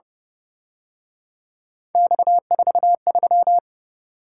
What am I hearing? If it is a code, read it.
Morse code, heard as a keyed tone, about 30 wpm, 700 Hz. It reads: X43
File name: Speed-Racing - Three Letter-Number Combinations - 25wpm